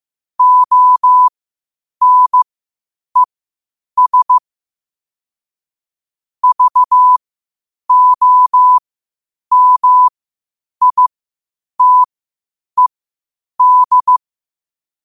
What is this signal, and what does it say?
Morse code, heard as a keyed tone, about 15 wpm, spaced as for 8 wpm, 1000 Hz. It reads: ONES VOMITED
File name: qotdfiles / 8wpm